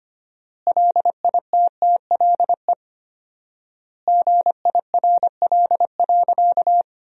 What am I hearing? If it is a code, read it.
Morse code, heard as a keyed tone, about 25 wpm, 700 Hz. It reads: LITTLE GIRL.
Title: 4x Word-Spacing - Sentences from Top 400 Words - 25wpm